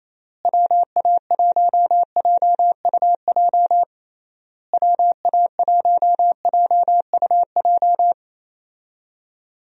Morse code, heard as a keyed tone, about 27 wpm, 700 Hz. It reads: WA1JUJ WA1JUJ
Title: USA Call Signs - 6 Characters - Repeated 3x - 28wpm